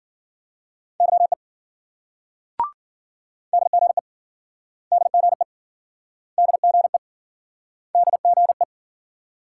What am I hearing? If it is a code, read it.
Morse code, heard as a keyed tone, about 56 wpm, 700 Hz. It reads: YE DGE DGE DGE DGE